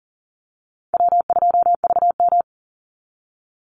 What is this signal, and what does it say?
Morse code, heard as a keyed tone, about 40 wpm, 700 Hz. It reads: W2VM
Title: Easy Call Signs Repeated 3x - 40wpm